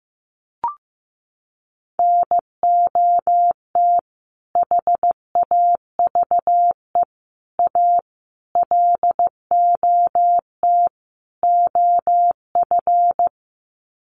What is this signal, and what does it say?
Morse code, heard as a keyed tone, about 15 wpm, 700 Hz. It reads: NOT HAVE A LOT OF
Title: Sets of 5 Words 15wpm